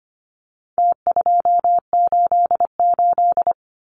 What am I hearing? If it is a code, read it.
Morse code, heard as a keyed tone, about 25 wpm, 700 Hz. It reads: T288
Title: Numbers - Contest Style - 4-Digit 25wpm